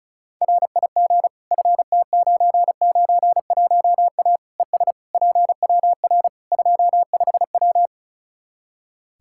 Morse code, heard as a keyed tone, about 36 wpm, 700 Hz. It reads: RIG FT991A ES PWR 25W
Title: Practice QSO - Exchange 2 - Info - 35wpm